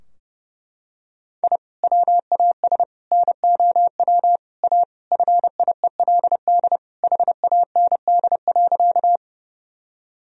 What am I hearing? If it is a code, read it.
Morse code, heard as a keyed tone, about 30 wpm, 700 Hz. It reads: I WAS NOW A FIELD HAND.
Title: Sentences from Top 1000 Words - Review Part 2 - 30wpm